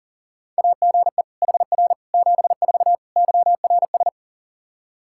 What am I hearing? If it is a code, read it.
Morse code, heard as a keyed tone, about 40 wpm, 700 Hz. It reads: AGE HR 74 YRS
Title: Practice QSO - Exchange 3 - More Info - 40wpm